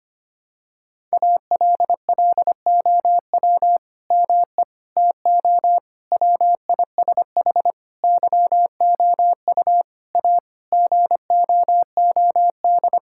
Morse code, heard as a keyed tone, about 25 wpm, 700 Hz. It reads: ALLOW ME TO WISH YOU A GOOD
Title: Sentences from Top 1600 Words 25wpm